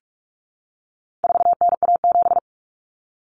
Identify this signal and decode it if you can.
Morse code, heard as a keyed tone, about 44 wpm, 700 Hz. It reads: 4NA7
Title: Common Cut Numbers - 4-Digit 45wpm